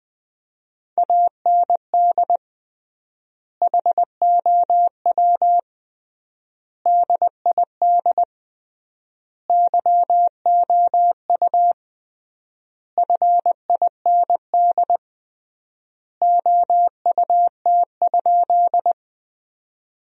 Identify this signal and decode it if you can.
Morse code, heard as a keyed tone, about 20 wpm, 700 Hz. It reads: AND HOW DID YOU FIND OUT?
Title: 3x Word-Spacing - Sentences from Top 200 Words - 20wpm